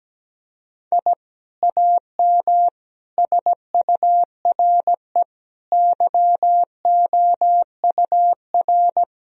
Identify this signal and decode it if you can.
Morse code, heard as a keyed tone, about 17 wpm, 700 Hz. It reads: I AM SURE YOUR
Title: Sentences from Top 1000 Words - Review Part 2 - 17wpm